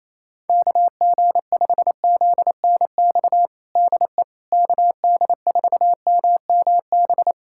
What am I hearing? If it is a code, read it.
Morse code, heard as a keyed tone, about 28 wpm, 700 Hz. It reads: KG5ZNX DE KD4MMB
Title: USA Callsign DE Callsign 28wpm